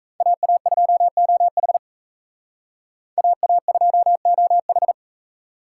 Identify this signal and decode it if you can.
Morse code, heard as a keyed tone, about 37 wpm, 700 Hz. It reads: AA2OH AA2OH